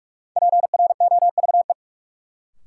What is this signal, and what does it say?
Morse code, heard as a keyed tone, about 45 wpm, 700 Hz. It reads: PROVE
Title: Sentences from Top 1600 Words - New Word Drill - 45wpm